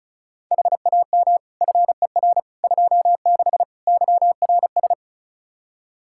Morse code, heard as a keyed tone, about 36 wpm, 700 Hz. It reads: HAM FER 26 YRS
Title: Practice QSO - Exchange 3 - More Info - 35wpm